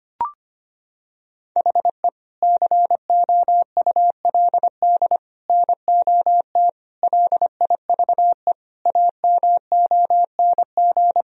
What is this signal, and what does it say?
Morse code, heard as a keyed tone, about 25 wpm, 700 Hz. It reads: HE COULD NOT LIVE AMONG
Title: Sentences from Top 1000 Words - Review Part 3 - 25wpm